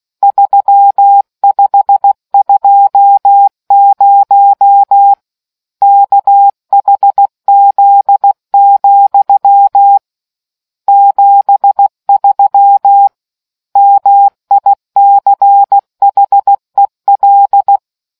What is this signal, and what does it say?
Morse code, heard as a keyed tone, about 16 wpm, 800 Hz. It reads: 3520 KHZ, 73 MICHEL